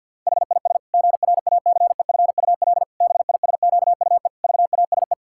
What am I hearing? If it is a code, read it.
Morse code, heard as a keyed tone, about 51 wpm, 700 Hz. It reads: HIS GRACEFUL BISQUE VASE
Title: Pangram Challenge v2 - 50wpm